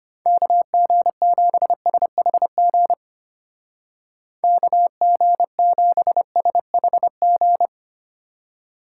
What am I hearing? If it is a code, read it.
Morse code, heard as a keyed tone, about 30 wpm, 700 Hz. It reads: KG7SHG KG7SHG
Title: Speed Racing - USA 2x3 Call Signs 25wpm